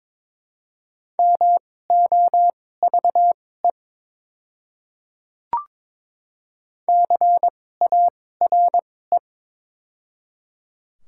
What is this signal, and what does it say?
Morse code, heard as a keyed tone, about 21 wpm, spaced as for 16 wpm, 700 Hz. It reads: MOVE CARE
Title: 2x Character Spacing - Top 300 Words - 22wpm